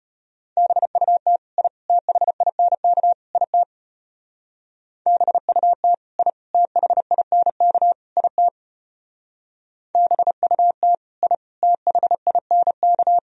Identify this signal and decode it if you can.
Morse code, heard as a keyed tone, about 33 wpm, 700 Hz. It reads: BUT I THINK IT BUT I THINK IT BUT I THINK